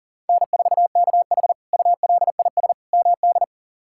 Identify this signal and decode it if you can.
Morse code, heard as a keyed tone, about 40 wpm, 700 Hz. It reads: N4KH ULIS MD
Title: K1USN SST QSO Practice - 40wpm